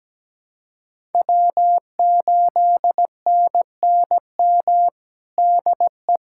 Morse code, heard as a keyed tone, about 17 wpm, 700 Hz. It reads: W8NNM DE
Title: USA Callsign DE Callsign - 5 Characters - 17wpm